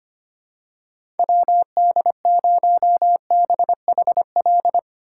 Morse code, heard as a keyed tone, about 25 wpm, 700 Hz. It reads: WD0BHL
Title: USA Call Signs - 3x Character Spacing - 25wpm